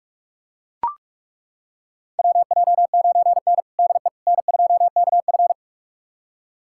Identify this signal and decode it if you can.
Morse code, heard as a keyed tone, about 41 wpm, 690 Hz. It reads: WJ9N DE N2KF
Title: USA Callsign DE Callsign - 4 Characters - 45wpm